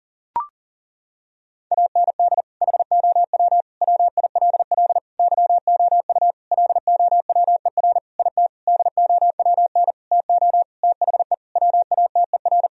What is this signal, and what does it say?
Morse code, heard as a keyed tone, about 40 wpm, 700 Hz. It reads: AND HOW WILL YOU LOWER IT DOWN TO THE WATER
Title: Sentences from Top 1000 Words - Review Part 4 - 40wpm